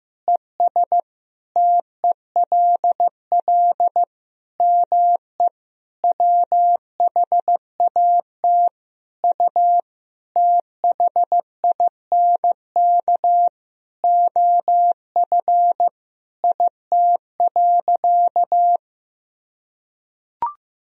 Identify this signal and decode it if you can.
Morse code, heard as a keyed tone, about 15 wpm, 700 Hz. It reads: ES TELL ME WHAT U THINK OF IT.
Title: Sentences from Top 200 Words and Abbreviations 15wpm